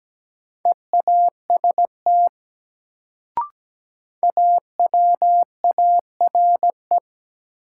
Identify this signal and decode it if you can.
Morse code, heard as a keyed tone, about 17 wpm, 700 Hz. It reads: EAST AWARE
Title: Sentences from Top 1200 Words - New Word Drill Alt - 17wpm